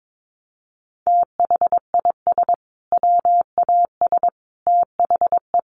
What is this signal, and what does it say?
Morse code, heard as a keyed tone, about 22 wpm, 700 Hz. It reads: THIS WAS THE
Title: Sentences from Top 900 Words 22wpm